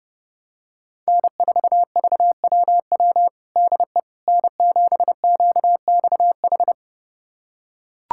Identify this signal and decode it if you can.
Morse code, heard as a keyed tone, about 32 wpm, 700 Hz. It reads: N4VWW DE N7QXH
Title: USA Callsign DE Callsign - 5 Characters - 30wpm